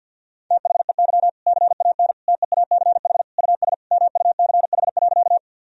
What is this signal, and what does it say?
Morse code, heard as a keyed tone, about 52 wpm, 700 Hz. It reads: THEY CAN TEACH US MUCH.
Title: Sentences from Top 1000 Words - Review Part 2 - 50wpm